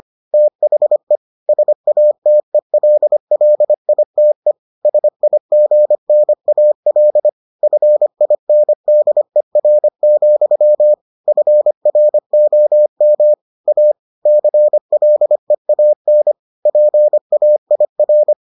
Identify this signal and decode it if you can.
Morse code, heard as a keyed tone, about 25 wpm, 600 Hz. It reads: THE SATELLITE SIGNAL FINDER, FROM A CLEAN PAIR